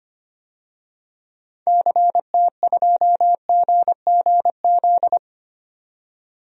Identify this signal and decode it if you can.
Morse code, heard as a keyed tone, about 25 wpm, 690 Hz. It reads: CT2GGZ
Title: DXCC-100 Call Signs - 6 Characters - 25wpm